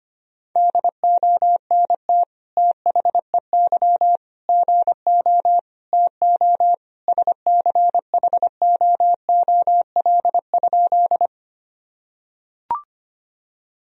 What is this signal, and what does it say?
Morse code, heard as a keyed tone, about 25 wpm, 700 Hz. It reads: DONT THEY GO TO SCHOOL?
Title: Sentences from Top 1000 Words - Review Part 1 - 25wpm